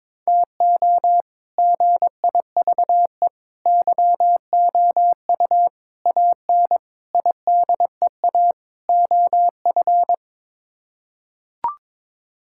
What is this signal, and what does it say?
Morse code, heard as a keyed tone, about 22 wpm, 700 Hz. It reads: TO GIVE YOU AN IDEA OF
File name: practice / 22wpm / Sets of 6 Words - 22wpm